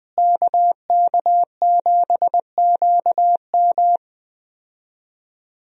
Morse code, heard as a keyed tone, about 20 wpm, 700 Hz. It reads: KK7QM
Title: USA Call Signs - 5 Characters - Repeated 3x - 20wpm